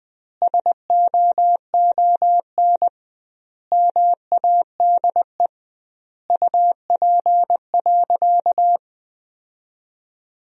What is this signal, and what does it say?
Morse code, heard as a keyed tone, about 20 wpm, 700 Hz. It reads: SOON MADE UP.
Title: 2x Word-Spacing - Sentences from Top 500 Words - 20wpm